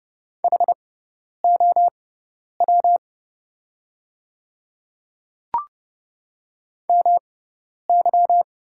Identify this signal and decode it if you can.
Morse code, heard as a keyed tone, about 30 wpm, spaced as for 10 wpm, 690 Hz. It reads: HOW MY
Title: 6x Character Spacing - Top 100 Words - 30wpm